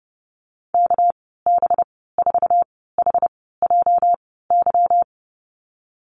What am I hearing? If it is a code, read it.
Morse code, heard as a keyed tone, about 32 wpm, spaced as for 17 wpm, 700 Hz. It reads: KB4HJY
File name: Lesson 37g - Letter x - Call Signs 30wpm